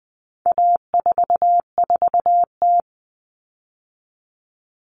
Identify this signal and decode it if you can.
Morse code, heard as a keyed tone, about 20 wpm, 700 Hz. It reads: A44T
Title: All Cut Numbers - 4-Digit 20wpm